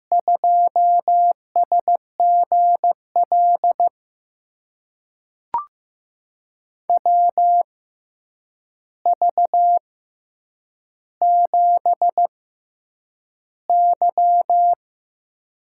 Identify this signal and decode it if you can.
Morse code, heard as a keyed tone, about 15 wpm, spaced as for 8 wpm, 700 Hz. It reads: 2SGL WV7Y
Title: USA Call Signs - 6x Character Spacing - 15wpm